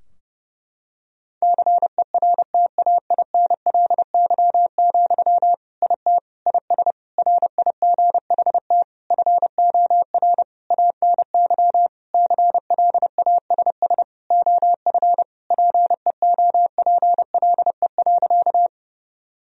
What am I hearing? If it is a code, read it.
Morse code, heard as a keyed tone, about 30 wpm, 700 Hz. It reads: CERTAINLY, IT IS RIGHT FOR ANY CLASS OF PEOPLE.
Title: Sentences from Top 1000 Words - Review Part 2 - 30wpm